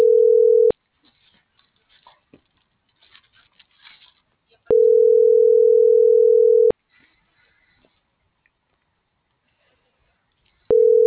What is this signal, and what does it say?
Ringback tone (North America), 440 + 480 Hz.